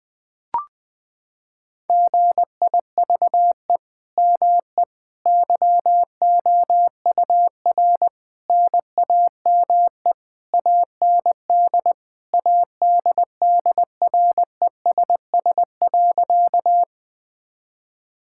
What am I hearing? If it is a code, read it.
Morse code, heard as a keyed tone, about 20 wpm, 700 Hz. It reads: GIVE ME YOUR NAME AND ADDRESS.
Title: Sentences from Top 1000 Words - Review Part 6 - 20wpm